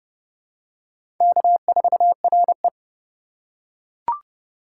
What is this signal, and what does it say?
Morse code, heard as a keyed tone, about 30 wpm, 690 Hz. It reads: K4RE
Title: USA Call Signs - 4 Characters - 2x Character Spacing - 30wpm